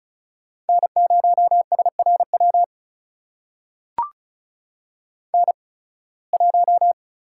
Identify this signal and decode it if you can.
Morse code, heard as a keyed tone, about 36 wpm, spaced as for 23 wpm, 700 Hz. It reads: N0SRW N1
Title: USA Call Signs - 5 Characters - 8x Character Spacing - 35wpm